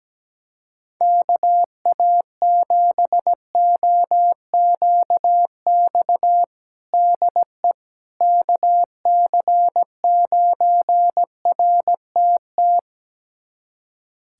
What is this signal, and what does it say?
Morse code, heard as a keyed tone, about 17 wpm, 700 Hz. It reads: KA7OQX DE KC9RTT
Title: USA Callsign DE Callsign - 6 Characters - 17wpm